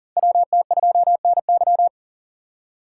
Morse code, heard as a keyed tone, about 40 wpm, 700 Hz. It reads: WT2NY
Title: USA Call Signs - 3x Character Spacing - 40wpm